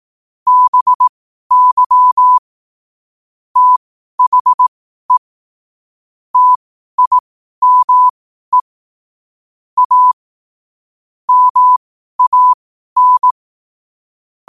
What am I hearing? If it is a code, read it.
Morse code, heard as a keyed tone, about 18 wpm, spaced as for 12 wpm, 1000 Hz. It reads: BY THE TIME A MAN